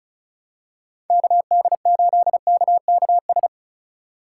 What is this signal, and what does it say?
Morse code, heard as a keyed tone, about 34 wpm, 700 Hz. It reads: KD8KKS
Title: USA Call Signs - 2x Character Spacing - 35wpm